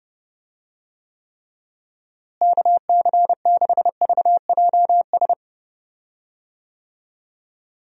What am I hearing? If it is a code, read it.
Morse code, heard as a keyed tone, about 32 wpm, 700 Hz. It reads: KC6VJS
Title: USA Call Signs - 6 Characters - Repeated 2x - 30wpm